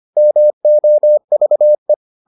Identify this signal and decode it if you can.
Morse code, heard as a keyed tone, about 25 wpm, 600 Hz. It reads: MOVE